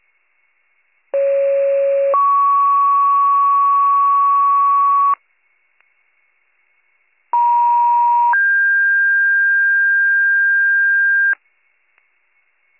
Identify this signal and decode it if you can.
Two-tone page, heard as a one-second tone, then a three-second tone, about 550 then 1120 Hz, about 950 then 1600 Hz. It is heard from 1.1 to 11.3 s.